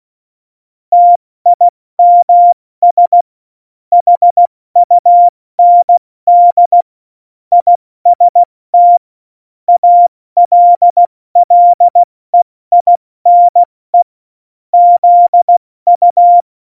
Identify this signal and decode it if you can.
Morse code, heard as a keyed tone, about 16 wpm, 700 Hz. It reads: TIMS HUND IST ALLEINE ZU
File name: Morsekurs / Witze / Witz 05. Juni 2018, 70 F80